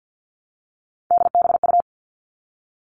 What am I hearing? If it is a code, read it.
Morse code, heard as a keyed tone, about 50 wpm, 700 Hz. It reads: DBU
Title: All Cut Numbers - 3-Digit 50wpm